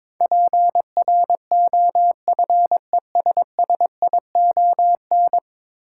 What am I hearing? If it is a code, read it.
Morse code, heard as a keyed tone, about 22 wpm, 700 Hz. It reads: PROFESSION
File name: practice / 22wpm / 10 Letter Words - First 3 Letters Given - 22wpm